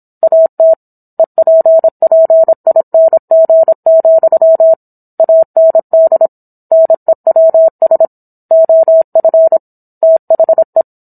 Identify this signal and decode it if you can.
Morse code, heard as a keyed tone, about 27 wpm, 650 Hz. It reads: AT EPPING, AND NEWS OF THE